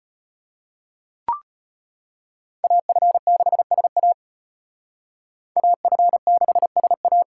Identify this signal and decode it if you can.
Morse code, heard as a keyed tone, about 32 wpm, 700 Hz. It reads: AF6SA AF6SA